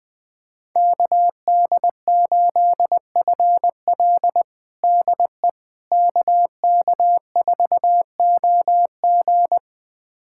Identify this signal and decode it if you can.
Morse code, heard as a keyed tone, about 20 wpm, 700 Hz. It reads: KD8FL DE KK4OG
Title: USA Callsign DE Callsign - 5 Characters - 20wpm